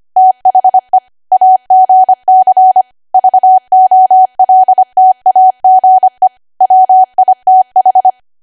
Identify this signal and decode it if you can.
Morse code, heard as a keyed tone, about 25 wpm, 750 Hz. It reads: THE AGC VOLTAGE WITH
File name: Archive / 25 WPM / 25WPM